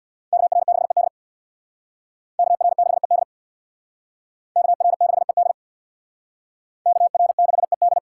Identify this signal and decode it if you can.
Morse code, heard as a keyed tone, about 62 wpm, 700 Hz. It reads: KR6ED KR6ED KR6ED KR6ED
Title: Speed Racing - USA 2x2 Call Signs 50wpm